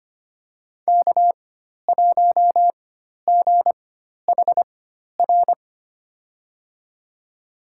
Morse code, heard as a keyed tone, about 25 wpm, spaced as for 12 wpm, 700 Hz. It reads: K1GHR
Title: USA Call Signs - 5 Characters - 4x Character Spacing - 25wpm